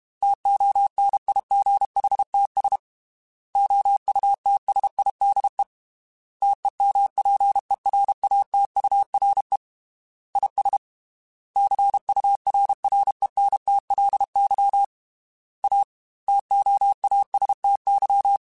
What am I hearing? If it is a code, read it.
Morse code, heard as a keyed tone, about 32 wpm, 800 Hz. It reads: TONIGHTS OUTSIDE TEMPERATURE IS CURRENTLY A TOASTY
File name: Weather Sets / Weather Sets-mp3s / Weather 5 12 @30